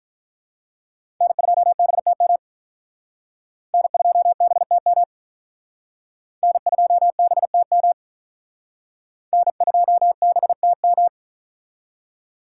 Morse code, heard as a keyed tone, about 47 wpm, 700 Hz. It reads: N2BTM N2BTM N2BTM N2BTM